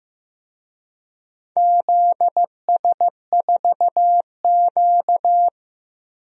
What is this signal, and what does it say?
Morse code, heard as a keyed tone, about 15 wpm, 700 Hz. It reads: ZS4Q
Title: DXCC-100 Call Signs - 4 Characters - 15wpm